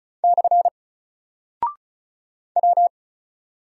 Morse code, heard as a keyed tone, about 36 wpm, 700 Hz. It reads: / W
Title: Single Character (Letter:Number:Punctuation:Prosign) Alt 35wpm